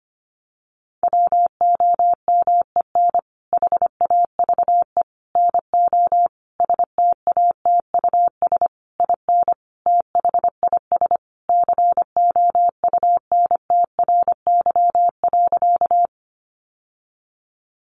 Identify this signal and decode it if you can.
Morse code, heard as a keyed tone, about 25 wpm, 700 Hz. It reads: WOMEN HAVE NO STATUS IN THIS COUNTRY.
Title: Sentences from Top 1400 Words 25wpm